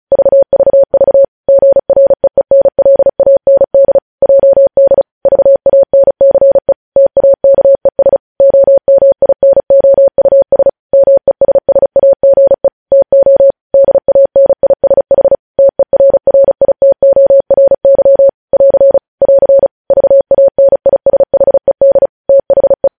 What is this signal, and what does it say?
Morse code, heard as a keyed tone, about 36 wpm, 550 Hz. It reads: VVV GREENLAND JD VANCE TAKES OMINOUS MESSAGE TO DANISH TERRITORY <AR> <AR> VANISHED THE
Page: News headlines in Morse code at 35 wpm